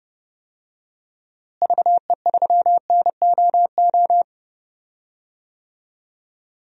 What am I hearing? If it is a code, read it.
Morse code, heard as a keyed tone, about 30 wpm, 700 Hz. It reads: VE3NOO
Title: DXCC-100 Call Signs - 6 Characters - Repeated 2x - 30wpm